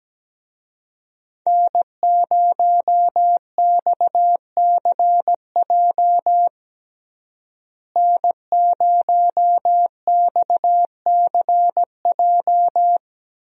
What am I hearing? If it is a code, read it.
Morse code, heard as a keyed tone, about 17 wpm, 700 Hz. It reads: N0XCJ N0XCJ
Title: USA Call Signs - 5 Characters - Repeated 3x - 17wpm